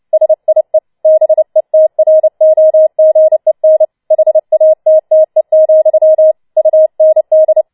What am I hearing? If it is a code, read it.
Morse code, heard as a keyed tone, about 29 wpm, 630 Hz. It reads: SIE BETROGEN HATTE, UND
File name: Schneewittchen Teil3 30wpm